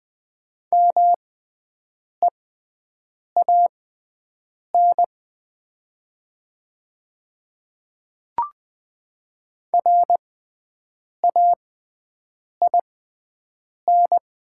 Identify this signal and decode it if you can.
Morse code, heard as a keyed tone, about 20 wpm, spaced as for 7 wpm, 700 Hz. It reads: MEAN RAIN
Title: 6x Character Spacing - Top 300 Words - 20wpm